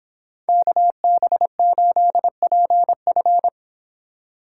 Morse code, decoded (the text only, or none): KB8PF